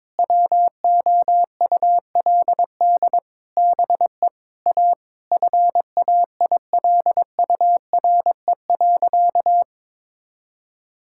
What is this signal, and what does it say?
Morse code, heard as a keyed tone, about 22 wpm, 700 Hz. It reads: WOULD BE A FAILURE.
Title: Sentences from Top 1000 Words - Review Part 6 - 22wpm